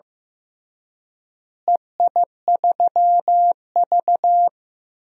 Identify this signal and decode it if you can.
Morse code, heard as a keyed tone, about 15 wpm, 700 Hz. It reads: EI3V
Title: DXCC-100 Call Signs - 4 Characters - Repeated 3x - 15wpm